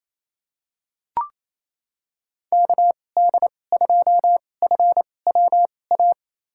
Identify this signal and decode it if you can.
Morse code, heard as a keyed tone, about 28 wpm, spaced as for 20 wpm, 690 Hz. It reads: KD2FWA
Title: USA Call Signs - 6 Characters - 2x Character Spacing - 28wpm